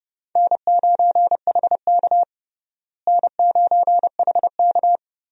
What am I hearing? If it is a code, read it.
Morse code, heard as a keyed tone, about 32 wpm, 700 Hz. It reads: N9HK N9HK
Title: USA Call Signs - 4 Characters - Repeated 3x - 30wpm